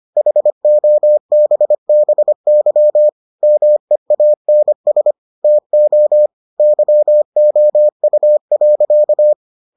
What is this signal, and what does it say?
Morse code, heard as a keyed tone, about 25 wpm, 600 Hz. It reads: HOBBY MEANS TO YOU.